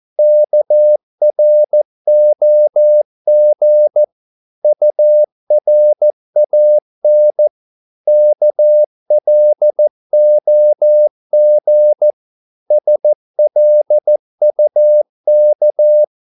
Morse code, heard as a keyed tone, about 14 wpm, 600 Hz. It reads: KROG URAN KLOG SLUK
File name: Korte ord DK 14wpm